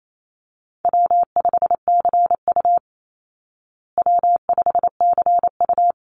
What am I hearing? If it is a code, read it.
Morse code, heard as a keyed tone, about 28 wpm, 700 Hz. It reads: W5CU W5CU
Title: Easy Call Signs Repeated 2x - 28wpm